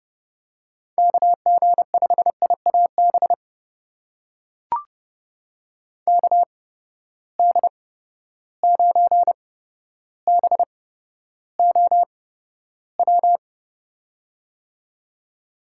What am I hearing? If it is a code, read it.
Morse code, heard as a keyed tone, about 30 wpm, spaced as for 13 wpm, 700 Hz. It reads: KG5IAB KD9BOW
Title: USA Call Signs - 6 Characters - 8x Character Spacing - 30wpm